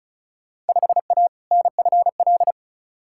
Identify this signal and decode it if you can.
Morse code, heard as a keyed tone, about 36 wpm, 700 Hz. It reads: 5A NFL
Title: Field Day Exchanges - 35wpm